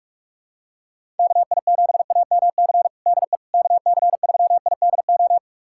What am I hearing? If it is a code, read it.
Morse code, heard as a keyed tone, about 47 wpm, 700 Hz. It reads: KI7AMC DE KC3IDO
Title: USA Callsign DE Callsign 45wpm